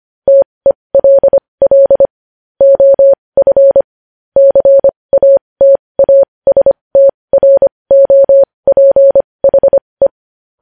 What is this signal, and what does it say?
Morse code, heard as a keyed tone, about 25 wpm, spaced as for 20 wpm, 550 Hz. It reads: TELL OF CATASTROPHE
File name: podcast / News headlines-1x-25x20